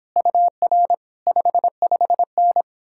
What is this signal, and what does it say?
Morse code, heard as a keyed tone, about 29 wpm, 700 Hz. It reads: UR 55N